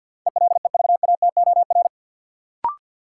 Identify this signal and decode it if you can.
Morse code, heard as a keyed tone, about 52 wpm, 700 Hz. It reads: ELEVATOR
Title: Sentences from Top 1600 Words - New Word Drill Alt - 50wpm